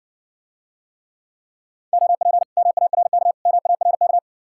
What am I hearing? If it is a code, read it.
Morse code, heard as a keyed tone, about 60 wpm, 700 Hz. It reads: OP MARK MARK